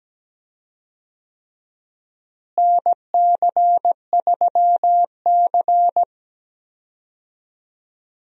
Morse code, heard as a keyed tone, about 17 wpm, 700 Hz. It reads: NC3C
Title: USA Call Signs - 4 Characters - Repeated 3x - 17wpm